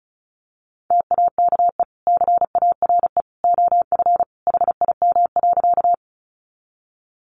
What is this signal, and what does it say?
Morse code, heard as a keyed tone, about 34 wpm, 700 Hz. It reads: TAKE CARE OF HIM.
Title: Sentences from Top 300 Words 35wpm